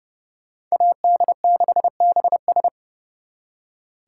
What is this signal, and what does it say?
Morse code, heard as a keyed tone, about 36 wpm, 700 Hz. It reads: AD6BS
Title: Speed Racing - USA Call Signs 30wpm